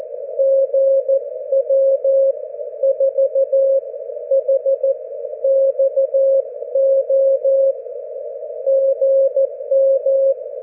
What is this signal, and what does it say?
Morse code, heard as a keyed tone, about 14 wpm, spaced as for 11 wpm, 540 Hz. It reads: GW4HXO GM
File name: GW3UEP / GW4HXO 40mWRF 42mls 24Jly2012